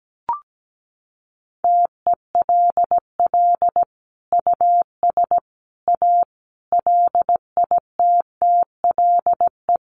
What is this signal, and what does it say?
Morse code, heard as a keyed tone, about 17 wpm, 700 Hz. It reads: TELL US A LITTLE
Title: Sets of 5 Words Encore2 17wpm